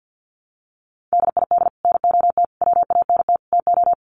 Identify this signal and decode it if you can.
Morse code, heard as a keyed tone, about 50 wpm, 700 Hz. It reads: DID NOT WANT TO
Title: Sets of 4 Words 50wpm